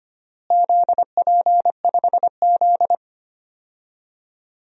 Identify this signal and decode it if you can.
Morse code, heard as a keyed tone, about 25 wpm, 700 Hz. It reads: ZP5Z
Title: DXCC-100 Call Signs - 4 Characters - Repeated 3x - 25wpm